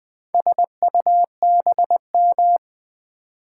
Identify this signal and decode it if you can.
Morse code, heard as a keyed tone, about 20 wpm, 700 Hz. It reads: SUBM
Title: Build A Word - 4 Letters At A Time - 10-Letter Words - 20wpm